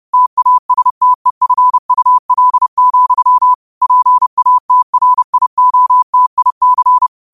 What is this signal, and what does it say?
Morse code, heard as a keyed tone, about 30 wpm, 1000 Hz. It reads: TASTEFUL, PATRIOTIC
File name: qotdfiles / 30wpm